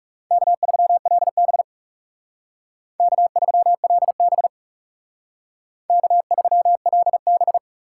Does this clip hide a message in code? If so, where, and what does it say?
Yes: Morse code from 0.3 to 7.6 s, heard as a keyed tone, about 38 wpm, 700 Hz. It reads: K3LB K3LB K3LB